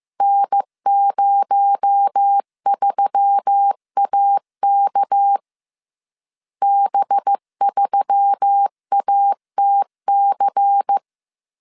Morse code, heard as a keyed tone, about 15 wpm, 800 Hz. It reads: N03AK B3ATC